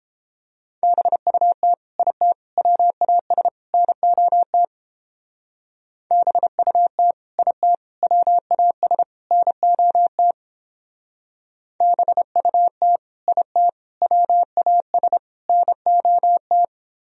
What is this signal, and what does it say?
Morse code, heard as a keyed tone, about 29 wpm, 700 Hz. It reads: BUT IT WAS NOT BUT IT WAS NOT BUT IT WAS NOT